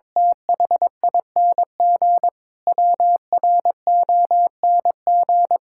Morse code, heard as a keyed tone, about 21 wpm, 700 Hz. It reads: THING WRONG
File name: Sentences with 7-Word Phrases - 22wpm